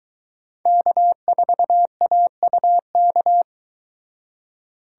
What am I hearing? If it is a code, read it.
Morse code, heard as a keyed tone, about 23 wpm, 700 Hz. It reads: K4AUK